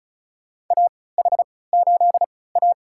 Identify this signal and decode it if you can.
Morse code, heard as a keyed tone, about 34 wpm, spaced as for 20 wpm, 700 Hz. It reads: AH8A
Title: USA Call Signs - 4 Characters - 3x Character Spacing - 35wpm